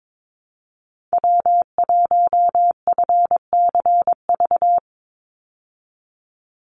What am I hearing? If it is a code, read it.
Morse code, heard as a keyed tone, about 22 wpm, 700 Hz. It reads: W1FCV
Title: Call Signs - 22wpm